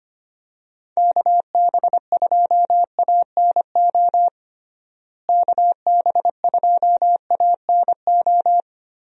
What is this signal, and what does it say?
Morse code, heard as a keyed tone, about 25 wpm, 700 Hz. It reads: KB2ANO KB2ANO
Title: USA Call Signs - 6 Characters - Repeated 3x - 25wpm